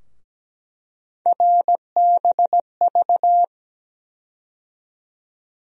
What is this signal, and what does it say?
Morse code, heard as a keyed tone, about 17 wpm, 700 Hz. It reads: RBV
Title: USA Call Sign Postfixes - 3 Letters 17wpm